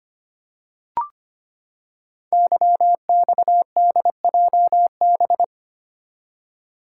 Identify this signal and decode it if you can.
Morse code, heard as a keyed tone, about 25 wpm, 700 Hz. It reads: YXDJB
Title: Code Groups - 5 Letters - 25wpm